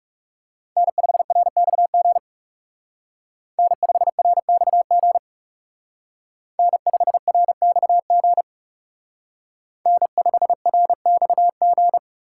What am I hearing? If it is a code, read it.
Morse code, heard as a keyed tone, about 36 wpm, 700 Hz. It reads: N5RXG N5RXG N5RXG N5RXG
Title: Speed Racing - USA 1x3 Call Signs 30wpm